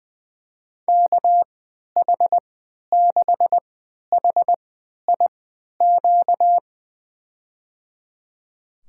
Morse code, heard as a keyed tone, about 20 wpm, spaced as for 11 wpm, 700 Hz. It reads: KH6HIQ
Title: USA Call Signs - 6 Characters - 3x Character Spacing - 20wpm